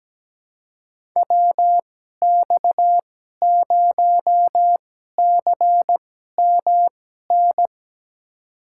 Morse code, heard as a keyed tone, about 17 wpm, spaced as for 12 wpm, 700 Hz. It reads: WX0CMN
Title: USA Call Signs - 2x Character Spacing - 17wpm